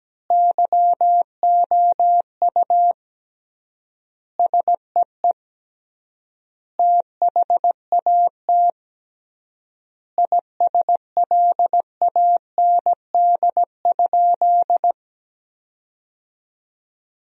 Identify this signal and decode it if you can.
Morse code, heard as a keyed tone, about 17 wpm, 700 Hz. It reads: YOU SEE THAT ISLAND?
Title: 3x Word-Spacing - Sentences from Top 500 Words - 17wpm